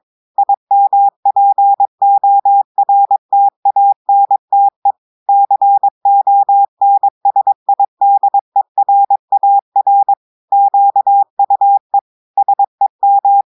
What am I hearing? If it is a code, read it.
Morse code, heard as a keyed tone, about 22 wpm, 800 Hz. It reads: IMPORTANTE CONSIDERAR QUE SEM
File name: audios / portugais / portugais texte en clair 2 vitesse 22 mots minutes